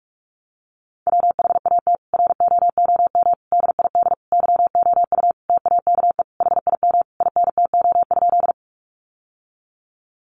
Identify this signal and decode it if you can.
Morse code, heard as a keyed tone, about 44 wpm, 700 Hz. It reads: WHAT ROOM DID YOU TAKE HIM INTO?
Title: Sentences from Top 600 Words 45wpm